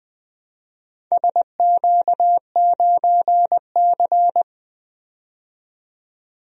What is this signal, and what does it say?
Morse code, heard as a keyed tone, about 20 wpm, 700 Hz. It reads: SQ9C
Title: DXCC-100 Call Signs - 4 Characters - 20wpm